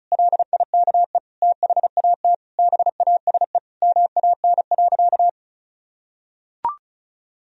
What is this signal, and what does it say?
Morse code, heard as a keyed tone, about 36 wpm, 700 Hz. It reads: LIKE THAT BASE MAN.
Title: Sentences from Top 1000 Words - Review Part 1 - 35wpm